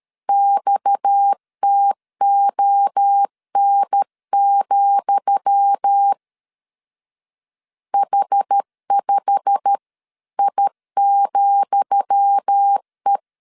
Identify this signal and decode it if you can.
Morse code, heard as a keyed tone, about 13 wpm, 800 Hz. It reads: XTON, H5I,E
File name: LEZIONI / 10 SIGNS / 10D COMA 12wpm 2544